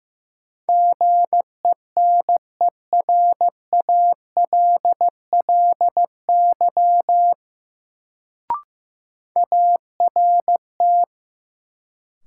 Morse code, heard as a keyed tone, about 16 wpm, 700 Hz. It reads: GENERALLY ART
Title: Sentences from Top 600 Words - New Word Drill - 15wpm